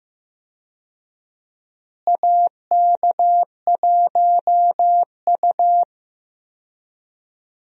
Morse code, heard as a keyed tone, about 15 wpm, 700 Hz. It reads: AK1U
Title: USA Call Signs - 4 Characters - Repeated 2x - 15wpm